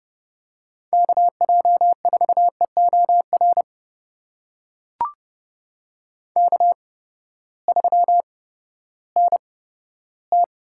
Morse code, heard as a keyed tone, about 30 wpm, spaced as for 15 wpm, 700 Hz. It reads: KJ4EOR K3NT
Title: USA Call Signs - 8x Character Spacing - 30wpm